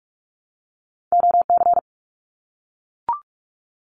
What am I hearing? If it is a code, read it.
Morse code, heard as a keyed tone, about 44 wpm, 700 Hz. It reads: OC
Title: Abbreviations 45wpm